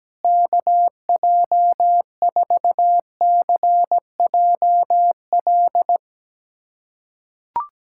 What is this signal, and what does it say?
Morse code, heard as a keyed tone, about 17 wpm, 700 Hz. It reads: KJ4CJL
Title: USA Callsign DE Callsign - 6 Characters - 17wpm